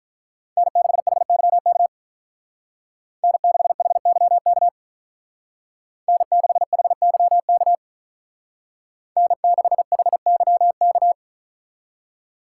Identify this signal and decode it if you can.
Morse code, heard as a keyed tone, about 44 wpm, 700 Hz. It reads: N6HYK N6HYK N6HYK N6HYK